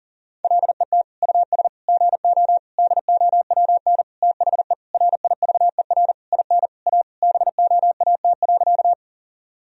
Morse code, heard as a keyed tone, about 40 wpm, 700 Hz. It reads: LET US GO DOWN THE RIVER IN A BOAT.
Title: Sentences from Top 1000 Words - Review Part 3 - 40wpm